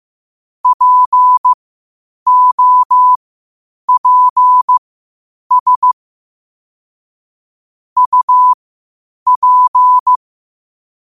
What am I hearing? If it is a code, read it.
Morse code, heard as a keyed tone, about 15 wpm, spaced as for 8 wpm, 1000 Hz. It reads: POPS UP
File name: qotdfiles / 8wpm